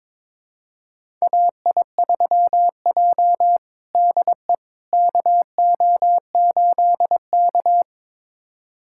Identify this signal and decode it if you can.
Morse code, heard as a keyed tone, about 22 wpm, 700 Hz. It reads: AI3J DE KO8K
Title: USA Callsign DE Callsign - 4 Characters - 22wpm